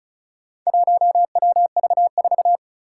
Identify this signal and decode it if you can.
Morse code, heard as a keyed tone, about 36 wpm, 690 Hz. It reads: 1WV4
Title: Four Challenging Letter-Number Combinations 35wpm